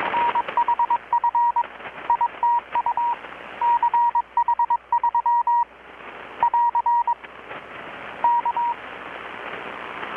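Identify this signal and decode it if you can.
Morse code, heard as a keyed tone, about 21 wpm, 950 Hz. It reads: NHF ITU CH3 <AR> K
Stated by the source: Morse from KPH's Night of Nights XX maritime radiotelegraphy event, received in Chiapas, Mexico about 3,800 km away on a Kenwood R-600 with a grounded 10 m long-wire antenna.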